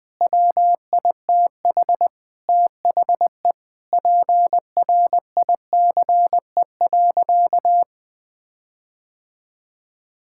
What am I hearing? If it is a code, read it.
Morse code, heard as a keyed tone, about 20 wpm, 700 Hz. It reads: WITH THE PRICE.
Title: Sentences with 7-Word Phrases - 20wpm